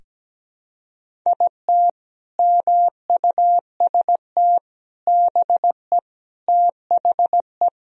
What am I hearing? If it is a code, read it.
Morse code, heard as a keyed tone, about 17 wpm, 700 Hz. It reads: IT MUST BE THE
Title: Sentences from Top 1000 Words - Review Part 1 - 17wpm